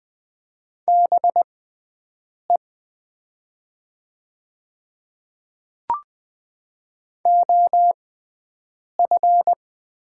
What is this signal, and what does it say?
Morse code, heard as a keyed tone, about 20 wpm, spaced as for 7 wpm, 690 Hz. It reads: BE OF
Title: 6x Character Spacing - Top 200 Words - 20wpm